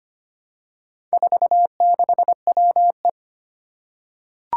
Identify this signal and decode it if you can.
Morse code, heard as a keyed tone, about 25 wpm, 700 Hz. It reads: 46WE
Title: Four Challenging Letter-Number Combinations 25wpm